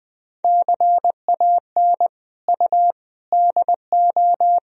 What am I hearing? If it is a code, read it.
Morse code, heard as a keyed tone, about 21 wpm, 700 Hz. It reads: CAN U DO
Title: Sentences from Top 100 Words and Abbreviations 20wpm